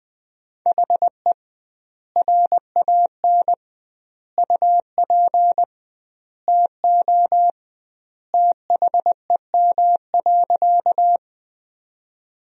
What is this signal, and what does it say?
Morse code, heard as a keyed tone, about 21 wpm, 700 Hz. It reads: HE RAN UP TO THEM.
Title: 2x Word-Spacing - Sentences from Top 500 Words - 20wpm